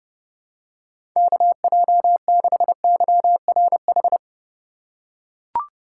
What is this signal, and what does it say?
Morse code, heard as a keyed tone, about 32 wpm, 700 Hz. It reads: KJ6YRH
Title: USA Call Signs - 6 Characters - 2x Character Spacing - 30wpm